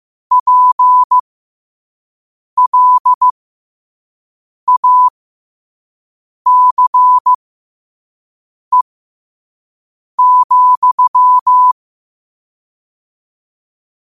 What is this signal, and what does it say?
Morse code, heard as a keyed tone, about 15 wpm, spaced as for 5 wpm, 1000 Hz. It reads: PLACE,
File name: qotdfiles / 5wpm